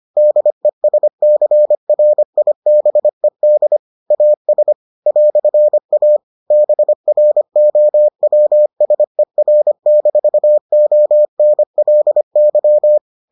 Morse code, heard as a keyed tone, about 25 wpm, 600 Hz. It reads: DESCRIBED AS "A BROWSER-ONLY